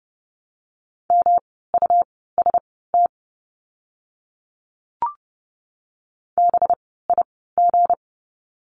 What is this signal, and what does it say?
Morse code, heard as a keyed tone, about 30 wpm, spaced as for 17 wpm, 700 Hz. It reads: MUST BIG
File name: course / 30:10wpm / 237 Lesson 38f - Letter q - Words 30wpm